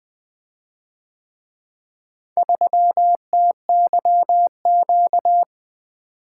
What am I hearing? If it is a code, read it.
Morse code, heard as a keyed tone, about 24 wpm, 700 Hz. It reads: 3TYQ
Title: Speed-Racing - Four Letter-Number Combinations - 20wpm